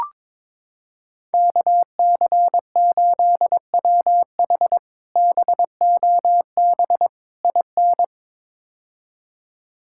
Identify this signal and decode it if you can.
Morse code, heard as a keyed tone, about 22 wpm, 700 Hz. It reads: KC8WH BOB IN
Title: K1USN SST QSO Practice - 22wpm